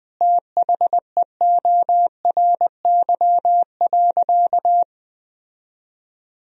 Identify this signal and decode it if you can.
Morse code, heard as a keyed tone, about 20 wpm, 700 Hz. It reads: THEORY.
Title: Sentences from Top 1000 Words - Review Part 3 - 20wpm